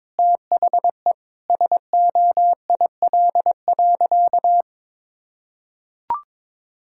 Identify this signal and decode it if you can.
Morse code, heard as a keyed tone, about 22 wpm, 700 Hz. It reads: THE SOIL.
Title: Sentences from Top 1000 Words - Review Part 5 - 22wpm